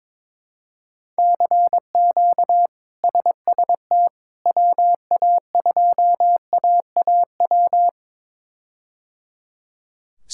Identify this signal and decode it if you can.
Morse code, heard as a keyed tone, about 22 wpm, 700 Hz. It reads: CQ SST WA2AAW
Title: K1USN SST QSO Practice - 22wpm